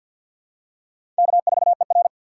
Morse code, heard as a keyed tone, about 52 wpm, 700 Hz. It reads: K4ER
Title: USA Call Signs - 4 Characters - 2x Character Spacing - 50wpm